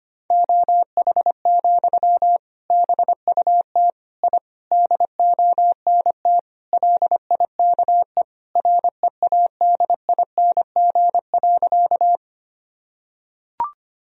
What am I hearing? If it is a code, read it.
Morse code, heard as a keyed tone, about 25 wpm, 700 Hz. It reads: OH, BUT I DONT LIKE READING.
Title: Sentences from Top 1000 Words - Review Part 3 - 25wpm